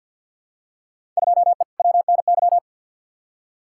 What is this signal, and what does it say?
Morse code, heard as a keyed tone, about 50 wpm, 700 Hz. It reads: 2E WNY
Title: Field Day Exchanges Encore - 50wpm